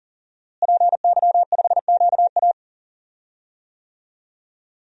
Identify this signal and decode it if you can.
Morse code, heard as a keyed tone, about 40 wpm, 700 Hz. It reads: PY5QA
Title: DXCC-100 Call Signs - 5 Characters - Repeated 3x - 40wpm